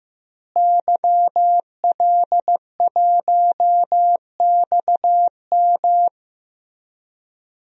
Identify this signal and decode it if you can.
Morse code, heard as a keyed tone, about 16 wpm, 700 Hz. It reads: YL1XM
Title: DXCC-100 Call Signs - 5 Characters - Repeated 2x - 15wpm